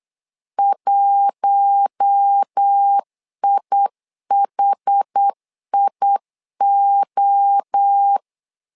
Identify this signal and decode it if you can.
Morse code, heard as a keyed tone, about 8 wpm, 800 Hz. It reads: 1IHIO
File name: LEZIONI / 4 Q1 8wpm 3035